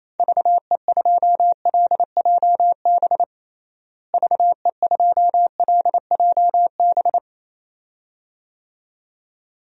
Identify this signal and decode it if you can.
Morse code, heard as a keyed tone, about 28 wpm, 700 Hz. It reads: VE2LJB VE2LJB
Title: DXCC-100 Call Signs - 6 Characters - Repeated 2x - 28wpm